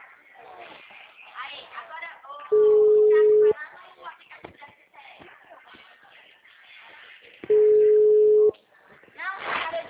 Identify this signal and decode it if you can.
Ringback tone (Europe, 425 Hz), with speech.